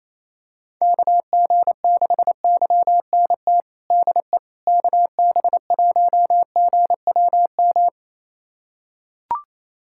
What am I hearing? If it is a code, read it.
Morse code, heard as a keyed tone, about 28 wpm, 700 Hz. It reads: KG6YNT DE KB1GWM
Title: USA Callsign DE Callsign - 6 Characters - 28wpm